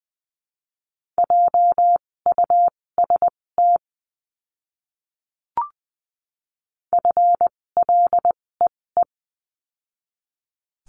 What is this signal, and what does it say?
Morse code, heard as a keyed tone, about 20 wpm, spaced as for 16 wpm, 700 Hz. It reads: JUST FLEE
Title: 256 Lesson 42a - Words 12-20wpm